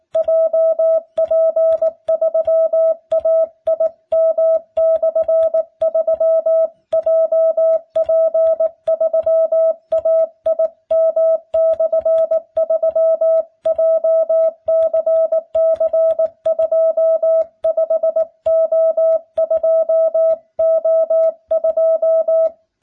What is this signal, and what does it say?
Morse code, heard as a keyed tone, about 19 wpm, 650 Hz. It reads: JP3AIM/3 JP3AIM/3 JCC25O2O2